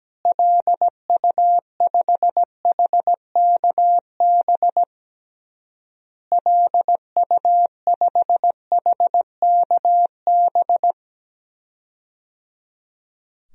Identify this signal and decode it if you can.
Morse code, heard as a keyed tone, about 17 wpm, 700 Hz. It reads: LU5HKB LU5HKB
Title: DXCC-100 Call Signs - 6 Characters - Repeated 3x - 17wpm